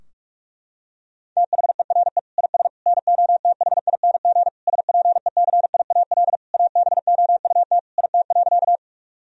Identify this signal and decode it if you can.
Morse code, heard as a keyed tone, about 41 wpm, 700 Hz. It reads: THERE IS NOTHING SPECIAL ABOUT IT.
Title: Sentences from Top 1000 Words - Review Part 3 - 45wpm